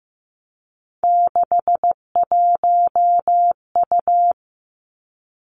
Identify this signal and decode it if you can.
Morse code, heard as a keyed tone, about 15 wpm, 700 Hz. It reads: 61U
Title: Three Challenging Letter-Number Combinations 15wpm